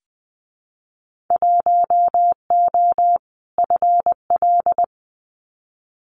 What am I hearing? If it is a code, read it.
Morse code, heard as a keyed tone, about 20 wpm, 700 Hz. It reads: 1O FL
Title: Winter Field Day Exchanges 20wpm